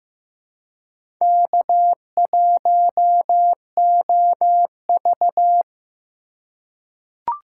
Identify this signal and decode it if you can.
Morse code, heard as a keyed tone, about 15 wpm, 700 Hz. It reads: K1OV
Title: USA Call Signs - 4 Characters - 2x Character Spacing - 15wpm